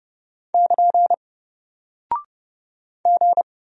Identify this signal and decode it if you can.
Morse code, heard as a keyed tone, about 30 wpm, 700 Hz. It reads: <KN> G
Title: Single Character (Letter:Number:Punctuation:Prosign) Alt 30wpm